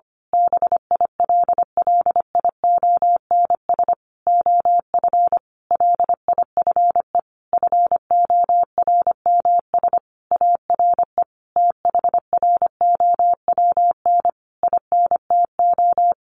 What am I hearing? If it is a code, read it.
Morse code, heard as a keyed tone, about 25 wpm, 700 Hz. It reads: BILLIONS OF LIFE FORMS ARE THROWN INTO
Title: A Sound Of Thunder 25wpm